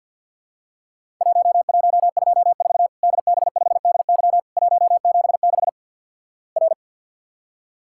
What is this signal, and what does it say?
Morse code, heard as a keyed tone, about 48 wpm, 700 Hz. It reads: 1124 DB5DY 176 R
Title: Worked All Europe DX Contest - QTC Exchange with Repeats - 50wpm